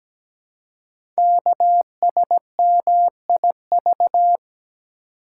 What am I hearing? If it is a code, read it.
Morse code, heard as a keyed tone, about 17 wpm, 700 Hz. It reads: KSMIV
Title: Code Groups - 5 Letters - 17wpm